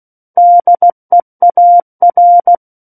Morse code, heard as a keyed tone, about 16 wpm, 700 Hz. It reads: DEAR
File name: 03-Chapter-03 16 WPM